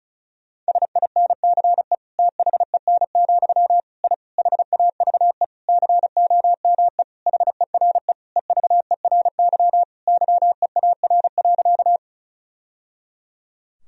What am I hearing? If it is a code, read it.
Morse code, heard as a keyed tone, about 36 wpm, 700 Hz. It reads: SINCE THEN, I HAVE COME HERE EVERY YEAR.
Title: Sentences from Top 1000 Words - Review Part 1 - 35wpm